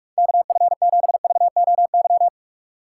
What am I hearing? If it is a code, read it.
Morse code, heard as a keyed tone, about 46 wpm, 690 Hz. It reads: KF7VOY
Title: USA Call Signs - 6 Characters - 2x Character Spacing - 45wpm